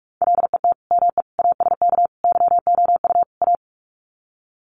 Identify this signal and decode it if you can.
Morse code, heard as a keyed tone, about 44 wpm, 700 Hz. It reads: LET ME ASK YOU A
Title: Sets of 5 Words Encore 45wpm